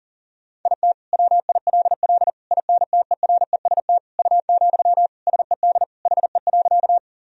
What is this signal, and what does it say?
Morse code, heard as a keyed tone, about 40 wpm, 700 Hz. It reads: IT WILL INTEREST U, SED HE.
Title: Sentences from Top 400 Words and Abbreviations 40wpm